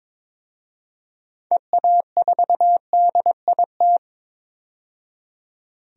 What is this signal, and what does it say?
Morse code, heard as a keyed tone, about 22 wpm, 700 Hz. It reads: EA4DIT
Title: DXCC-100 Call Signs - 6 Characters - 22wpm